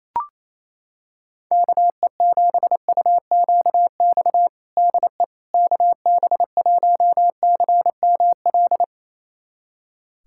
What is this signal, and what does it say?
Morse code, heard as a keyed tone, about 28 wpm, 700 Hz. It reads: KE7UQX DE KB1CML
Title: USA Callsign DE Callsign 28wpm